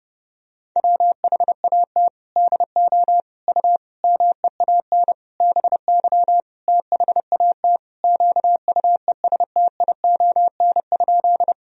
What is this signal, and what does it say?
Morse code, heard as a keyed tone, about 30 wpm, 700 Hz. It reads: WHAT DO U MEAN BY THAT QUESTION?
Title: Sentences from Top 400 Words and Abbreviations 30wpm